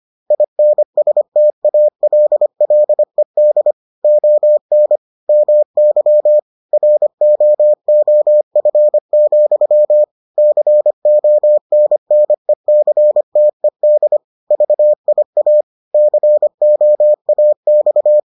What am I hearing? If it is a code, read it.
Morse code, heard as a keyed tone, about 25 wpm, 600 Hz. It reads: INSTALLED ON MY ROOF, CONNECTED VIA COAX